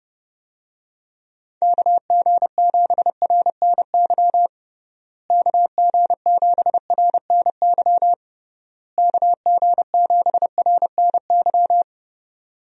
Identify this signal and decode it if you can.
Morse code, heard as a keyed tone, about 28 wpm, 700 Hz. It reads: KG7RNY KG7RNY KG7RNY
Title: USA Call Signs - 6 Characters - Repeated 3x - 30wpm